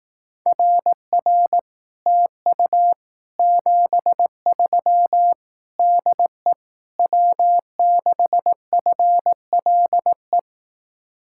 Morse code, heard as a keyed tone, about 18 wpm, 700 Hz. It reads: RR TU 73 DE W6FLE
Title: Speed-Racing Common QSO Phrases Encore - 15wpm